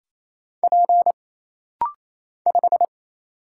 Morse code, heard as a keyed tone, about 28 wpm, 700 Hz. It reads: P 5
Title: Single Letter-Number Alt - Rapid-Fire 28wpm